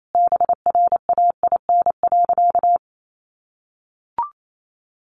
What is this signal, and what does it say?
Morse code, heard as a keyed tone, about 28 wpm, 700 Hz. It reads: BRAIN.
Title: Sentences from Top 1100 Words 28wpm